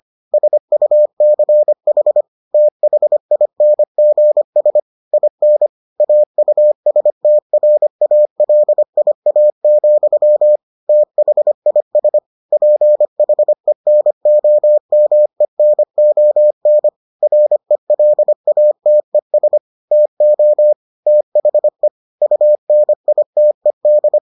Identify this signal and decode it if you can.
Morse code, heard as a keyed tone, about 25 wpm, 600 Hz. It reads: SUCH THINGS IN AUSTRALIA, THIS PHENOMENON RELATES TO THE UNITED